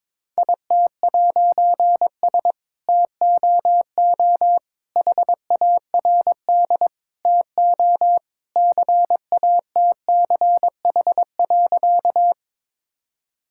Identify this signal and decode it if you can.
Morse code, heard as a keyed tone, about 22 wpm, 690 Hz. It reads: IT'S TOO HARD TO CATCH.
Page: morse-code-ninja-practice-22wpm